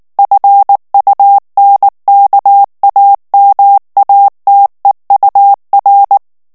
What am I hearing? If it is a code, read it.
Morse code, heard as a keyed tone, about 19 wpm, 800 Hz. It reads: FUNKAMATEUR
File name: morse mp3 / ALLEMAND texte clair 2 vitesse 18 mots minute